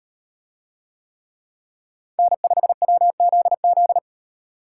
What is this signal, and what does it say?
Morse code, heard as a keyed tone, about 38 wpm, 700 Hz. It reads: N5WZZ